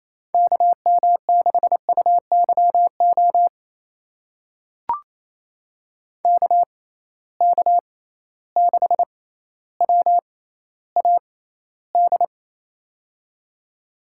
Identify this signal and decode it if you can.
Morse code, heard as a keyed tone, about 28 wpm, spaced as for 14 wpm, 700 Hz. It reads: KM6UYO KK6WAD
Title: USA Call Signs - 6 Characters - 6x Character Spacing - 28wpm